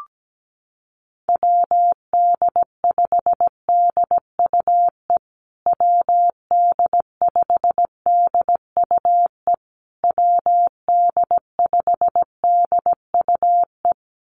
Morse code, heard as a keyed tone, about 17 wpm, 700 Hz. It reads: WD5DUE WD5DUE WD5DUE
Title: USA Call Signs Repeated 3x 17wpm